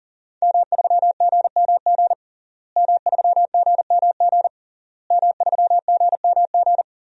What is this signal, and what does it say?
Morse code, heard as a keyed tone, about 40 wpm, 700 Hz. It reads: M3GMG M3GMG M3GMG
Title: DXCC-100 Call Signs - 5 Characters - Repeated 3x - 40wpm